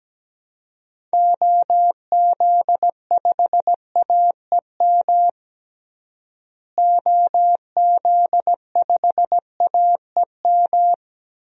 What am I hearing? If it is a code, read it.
Morse code, heard as a keyed tone, about 17 wpm, 700 Hz. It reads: OZ5AEM OZ5AEM
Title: DXCC-100 Call Signs - 6 Characters - Repeated 3x - 17wpm